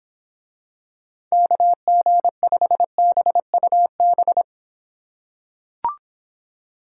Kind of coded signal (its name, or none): Morse code